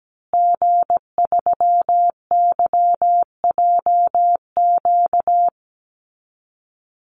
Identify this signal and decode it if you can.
Morse code, heard as a keyed tone, about 17 wpm, 700 Hz. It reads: G3YJQ
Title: Call Signs Repeated 3x - 17wpm